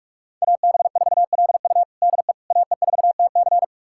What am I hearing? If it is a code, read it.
Morse code, heard as a keyed tone, about 46 wpm, 700 Hz. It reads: AB4LU DE AE4TC
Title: USA Callsign DE Callsign - 5 Characters - 45wpm